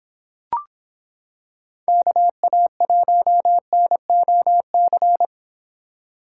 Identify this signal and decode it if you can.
Morse code, heard as a keyed tone, about 26 wpm, 700 Hz. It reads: KA1NOC